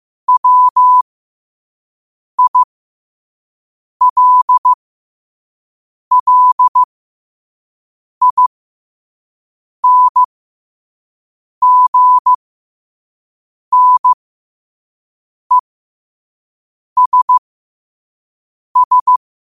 Morse code, heard as a keyed tone, about 15 wpm, spaced as for 5 wpm, 1000 Hz. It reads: WILLINGNESS
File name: qotdfiles / 5wpm